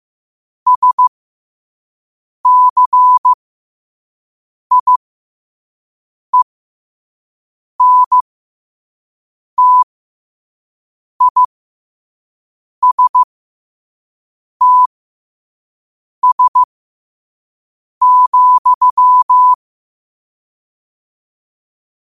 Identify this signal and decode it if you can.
Morse code, heard as a keyed tone, about 15 wpm, spaced as for 5 wpm, 1000 Hz. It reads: SCIENTISTS,